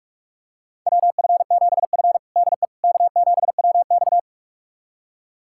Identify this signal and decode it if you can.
Morse code, heard as a keyed tone, about 46 wpm, 700 Hz. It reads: WF7F DE K7WX
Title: USA Callsign DE Callsign - 4 Characters - 45wpm